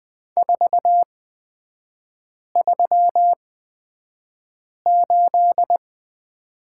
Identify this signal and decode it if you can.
Morse code, heard as a keyed tone, about 20 wpm, spaced as for 5 wpm, 700 Hz. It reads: 438
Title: 255 Lesson 41e - Period - Three Characters 20wpm